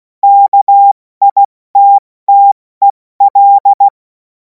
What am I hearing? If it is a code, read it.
Morse code, heard as a keyed tone, about 16 wpm, spaced as for 14 wpm, 800 Hz. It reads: KITTEL
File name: Morsekurs / Witze / Witz 23. Juni 2026 70 F80